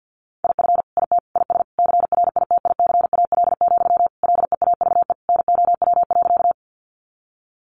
Morse code, heard as a keyed tone, about 50 wpm, 700 Hz. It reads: IF IT IS CRITICAL, LEAVE NOW.
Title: Sentences from Top 1100 Words 50wpm